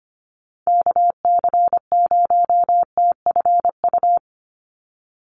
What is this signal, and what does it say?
Morse code, heard as a keyed tone, about 25 wpm, 690 Hz. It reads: KC0TFU
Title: USA Call Signs 25wpm